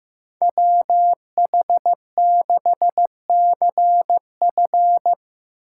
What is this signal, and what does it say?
Morse code, heard as a keyed tone, about 15 wpm, 700 Hz. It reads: WH6CF
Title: USA Call Signs - 8x Character Spacing - 15wpm